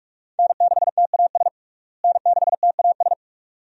Morse code, heard as a keyed tone, about 46 wpm, 700 Hz. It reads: N6TAS N6TAS
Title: USA Call Signs - 5 Characters - Repeated 2x - 45wpm